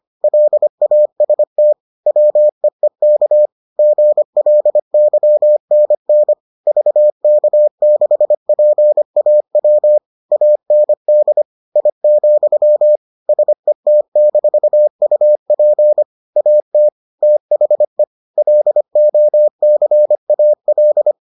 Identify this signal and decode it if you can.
Morse code, heard as a keyed tone, about 25 wpm, 600 Hz. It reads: LAST WEEK GLYNN VK6PAW AND I, SET-UP AT THE LOCAL